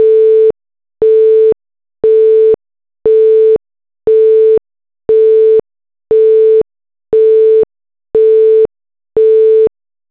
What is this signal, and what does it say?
Busy tone (France), 440 Hz.
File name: 2n Netstar / busytones